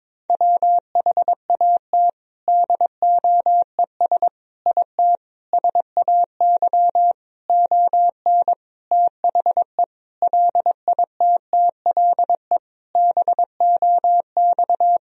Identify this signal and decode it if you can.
Morse code, heard as a keyed tone, about 22 wpm, 700 Hz. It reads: WHAT DOES IT SAY ON THE LITTLE BOX
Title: Sentences from Top 1000 Words - Review Part 2 - 22wpm